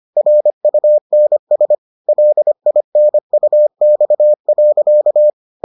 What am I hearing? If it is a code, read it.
Morse code, heard as a keyed tone, about 25 wpm, 600 Hz. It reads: RUNS LINUX.